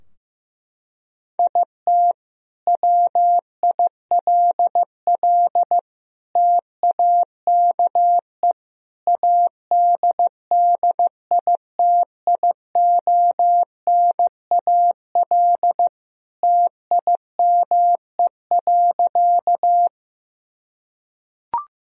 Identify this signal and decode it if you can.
Morse code, heard as a keyed tone, about 16 wpm, 700 Hz. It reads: IT WILL TAKE ADDITIONAL TIME.
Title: Sentences from Top 1000 Words - Review Part 4 - 15wpm